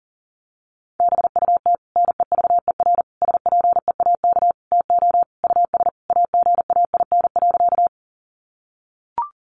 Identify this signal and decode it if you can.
Morse code, heard as a keyed tone, about 40 wpm, 700 Hz. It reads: BUT NEVER SPEAK TO US AGAIN.
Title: Sentences from Top 1100 Words 40wpm